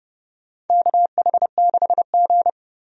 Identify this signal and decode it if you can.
Morse code, heard as a keyed tone, about 30 wpm, 700 Hz. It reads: KH6G
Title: DXCC-100 Call Signs - 4 Characters - 30wpm